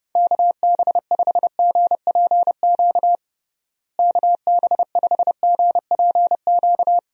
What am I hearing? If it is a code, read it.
Morse code, heard as a keyed tone, about 30 wpm, 700 Hz. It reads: KB5GPQ KB5GPQ
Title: USA Call Signs - 6 Characters - Repeated 3x - 30wpm